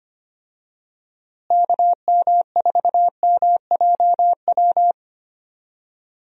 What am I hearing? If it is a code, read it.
Morse code, heard as a keyed tone, about 25 wpm, 700 Hz. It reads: KM4MJW
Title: USA Call Signs - 6 Characters - Repeated 3x - 25wpm